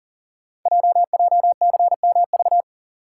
Morse code, heard as a keyed tone, about 43 wpm, 700 Hz. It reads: JJCMV
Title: Code Groups - 5 Letters - 40wpm